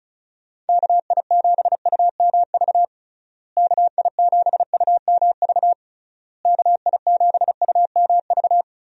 Morse code, heard as a keyed tone, about 36 wpm, 700 Hz. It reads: KI7UMV KI7UMV KI7UMV
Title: USA Call Signs - 6 Characters - Repeated 3x - 35wpm